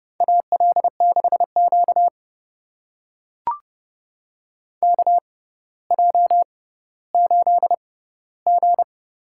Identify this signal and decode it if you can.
Morse code, heard as a keyed tone, about 30 wpm, spaced as for 15 wpm, 700 Hz. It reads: AL6Q KJ8G
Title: USA Call Signs - 4 Characters - 6x Character Spacing - 30wpm